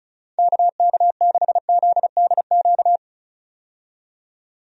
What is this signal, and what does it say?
Morse code, heard as a keyed tone, about 36 wpm, 700 Hz. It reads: KK6ZDQ
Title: USA Call Signs - 6 Characters - 35wpm